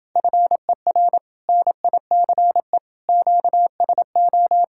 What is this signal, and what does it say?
Morse code, heard as a keyed tone, about 29 wpm, 700 Hz. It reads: FER NICE QSO